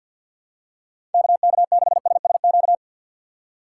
Morse code, heard as a keyed tone, about 50 wpm, 700 Hz. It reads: KK6SSQ
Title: USA Call Signs - 4x Character Spacing - 50wpm